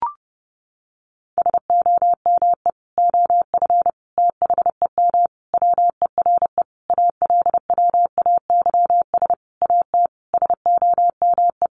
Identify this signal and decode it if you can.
Morse code, heard as a keyed tone, about 30 wpm, 700 Hz. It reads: SOME OF THEM WERE ALWAYS AT SOME
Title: Sentences from Top 1200 Words 30wpm